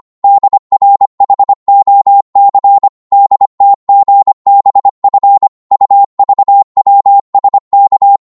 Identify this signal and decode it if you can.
Morse code, heard as a keyed tone, about 25 wpm, 800 Hz. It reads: DRHOC DTGBF UVWSK
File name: Morse / ltr-25wpm